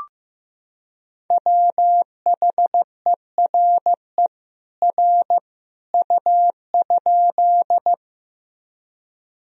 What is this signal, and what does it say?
Morse code, heard as a keyed tone, about 15 wpm, 700 Hz. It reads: WHERE R U?
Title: Sentences from Top 200 Words and Abbreviations 15wpm